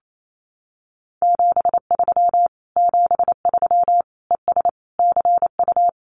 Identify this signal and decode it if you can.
Morse code, heard as a keyed tone, about 28 wpm, 700 Hz. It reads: 73 73 ES CU
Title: Common QSO Phrases Encore 28wpm